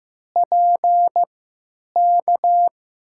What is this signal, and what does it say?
Morse code, heard as a keyed tone, about 15 wpm, spaced as for 9 wpm, 700 Hz. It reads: PK
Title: USA Call Signs - 3x Character Spacing - 15wpm